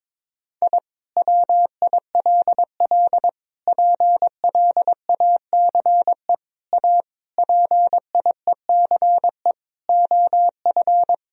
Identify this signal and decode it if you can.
Morse code, heard as a keyed tone, about 22 wpm, 700 Hz. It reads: I WILL PLACE A PIECE OF
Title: Sentences from Top 1000 Words - Review Part 2 - 22wpm